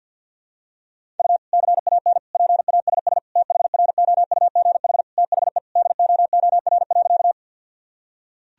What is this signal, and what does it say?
Morse code, heard as a keyed tone, about 53 wpm, 700 Hz. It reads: U CAN PASS THROUGH THE DOOR.
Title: Sentences from Top 400 Words and Abbreviations 50wpm